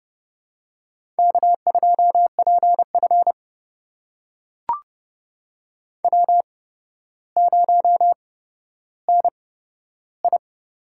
Morse code, heard as a keyed tone, about 30 wpm, spaced as for 13 wpm, 700 Hz. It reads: K2PF W0NI
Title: USA Call Signs - 4 Characters - 8x Character Spacing - 30wpm